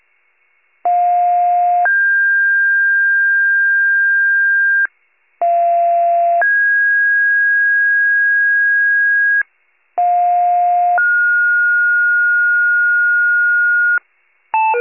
Two-tone page, heard as a one-second tone, then a three-second tone, about 710 then 1600 Hz, about 690 then 1650 Hz, about 710 then 1430 Hz. It is heard from 0.8 to 14.0 s.